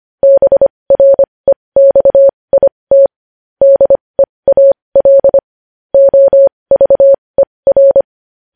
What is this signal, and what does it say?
Morse code, heard as a keyed tone, about 25 wpm, spaced as for 20 wpm, 550 Hz. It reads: BREXIT DEAL OVER
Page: News Headlines in Morse Code at 20 WPM